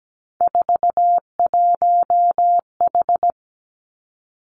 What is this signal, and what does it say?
Morse code, heard as a keyed tone, about 17 wpm, 700 Hz. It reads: 41H
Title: Three Challenging Letter-Number Combinations 17wpm